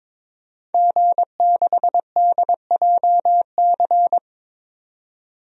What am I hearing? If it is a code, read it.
Morse code, heard as a keyed tone, about 22 wpm, 700 Hz. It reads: G6DJC
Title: DXCC-100 Call Signs - 5 Characters - 22wpm